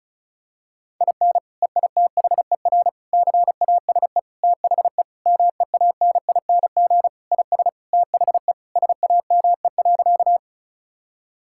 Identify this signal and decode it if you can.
Morse code, heard as a keyed tone, about 36 wpm, 700 Hz. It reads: IN EITHER CASE THE MEANING IS THE SAME.
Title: Sentences from Top 1000 Words - Review Part 6 - 35wpm